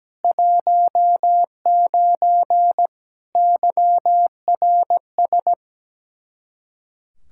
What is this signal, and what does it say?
Morse code, heard as a keyed tone, about 17 wpm, 700 Hz. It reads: 19 YRS
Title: Practice QSO - Exchange 3 - More Info - 17wpm